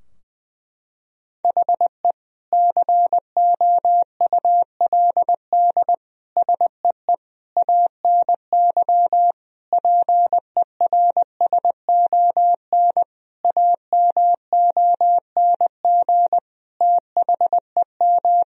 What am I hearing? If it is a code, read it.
Morse code, heard as a keyed tone, about 20 wpm, 700 Hz. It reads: HE COULD SEE ANY PERSON AMONG THEM
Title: Sentences from Top 1000 Words - Review Part 3 - 20wpm